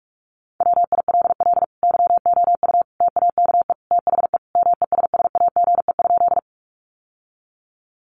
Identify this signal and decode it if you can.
Morse code, heard as a keyed tone, about 44 wpm, 700 Hz. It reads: WILL YOU TAKE THE MESSAGE?
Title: Sentences from Top 1100 Words 45wpm